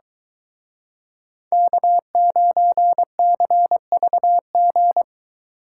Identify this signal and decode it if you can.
Morse code, heard as a keyed tone, about 24 wpm, 700 Hz. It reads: K9CVG